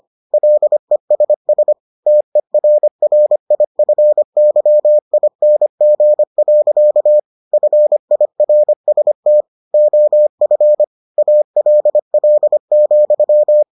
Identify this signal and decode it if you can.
Morse code, heard as a keyed tone, about 25 wpm, 600 Hz. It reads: LESS TERRIFYING. FIRST OF ALL,